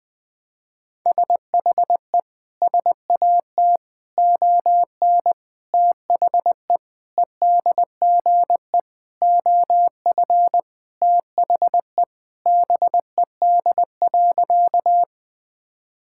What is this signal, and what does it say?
Morse code, heard as a keyed tone, about 20 wpm, 700 Hz. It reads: SHE SAT ON THE EDGE OF THE BED.
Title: Sentences with 4-Word Phrases - 20wpm